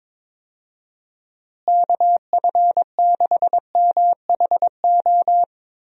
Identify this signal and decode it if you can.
Morse code, heard as a keyed tone, about 22 wpm, 700 Hz. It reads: KF6MHO
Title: USA Call Signs - 6 Characters - Repeated 2x - 22wpm